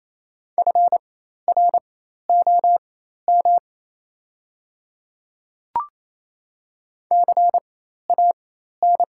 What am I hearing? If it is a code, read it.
Morse code, heard as a keyed tone, about 28 wpm, spaced as for 13 wpm, 700 Hz. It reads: FROM CAN
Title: 4x Character Spacing - Top 100 Words - 28wpm